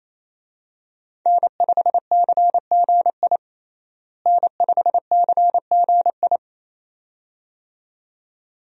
Morse code, heard as a keyed tone, about 27 wpm, 700 Hz. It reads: N5CGI N5CGI
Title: USA Call Signs - 5 Characters - Repeated 2x - 28wpm